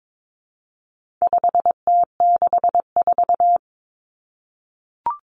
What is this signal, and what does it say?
Morse code, heard as a keyed tone, about 22 wpm, 700 Hz. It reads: 5T64
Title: Common Cut Numbers - 4-Digit 22wpm